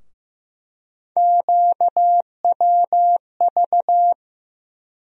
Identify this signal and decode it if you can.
Morse code, heard as a keyed tone, about 15 wpm, 700 Hz. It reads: QWV
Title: USA Call Sign Postfixes - 3 Letters 15wpm